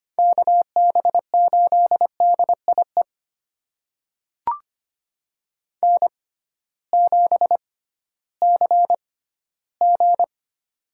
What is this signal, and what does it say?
Morse code, heard as a keyed tone, about 25 wpm, spaced as for 14 wpm, 700 Hz. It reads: KB8DIE N7CG
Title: USA Call Signs - 6x Character Spacing - 25wpm